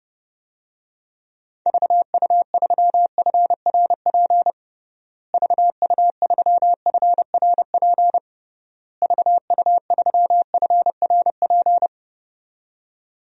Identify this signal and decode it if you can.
Morse code, heard as a keyed tone, about 30 wpm, 700 Hz. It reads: VU3FRP VU3FRP VU3FRP
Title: DXCC-100 Call Signs - 6 Characters - Repeated 3x - 30wpm